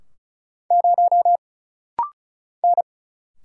Morse code, heard as a keyed tone, about 36 wpm, 700 Hz. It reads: 0 N
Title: Single Letter-Number Alt - Rapid-Fire 35wpm